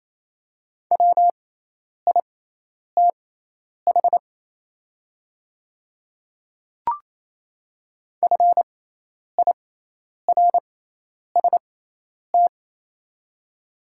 Morse code, heard as a keyed tone, about 28 wpm, spaced as for 10 wpm, 690 Hz. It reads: WITH FIRST
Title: 6x Character Spacing - Top 100 Words - 28wpm